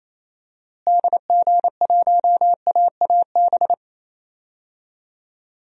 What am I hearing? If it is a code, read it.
Morse code, heard as a keyed tone, about 28 wpm, 700 Hz. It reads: DG1AAB
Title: DXCC-100 Call Signs - 6 Characters - Repeated 2x - 28wpm